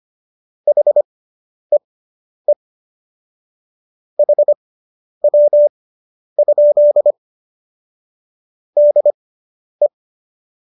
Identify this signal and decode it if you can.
Morse code, heard as a keyed tone, about 25 wpm, spaced as for 10 wpm, 590 Hz. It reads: HEE HW? DE